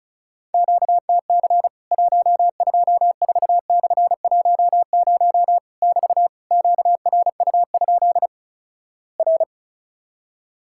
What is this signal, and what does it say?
Morse code, heard as a keyed tone, about 36 wpm, 700 Hz. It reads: QTC 124/10 <BT> QRU? R
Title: Worked All Europe DX Contest - QTC Exchange - 35wpm